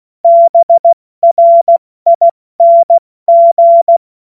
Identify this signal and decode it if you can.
Morse code, heard as a keyed tone, about 16 wpm, spaced as for 14 wpm, 680 Hz. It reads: BRING
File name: Morsekurs / Witze / Witz 24. August 2021, 70 F80